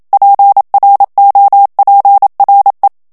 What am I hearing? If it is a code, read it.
Morse code, heard as a keyed tone, about 27 wpm, 800 Hz. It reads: PROPRE
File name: FRANCAIS TEXTE CLAIR 2 VITESSE 26 MOTS MINUTES